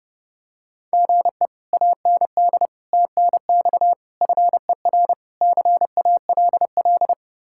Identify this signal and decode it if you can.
Morse code, heard as a keyed tone, about 29 wpm, 700 Hz. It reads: GE AND TNX FER CALL